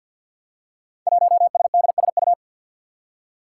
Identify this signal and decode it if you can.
Morse code, heard as a keyed tone, about 51 wpm, 700 Hz. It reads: 1SDSU
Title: Five Challenging Letter-Number Combinations 50wpm